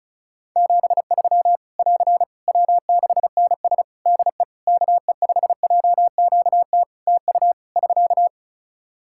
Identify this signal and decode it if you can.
Morse code, heard as a keyed tone, about 36 wpm, 700 Hz. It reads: 73 <AR> W6NS DE KE5JQT TU <SK>
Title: Practice QSO - Exchange 4 - Ending - 35wpm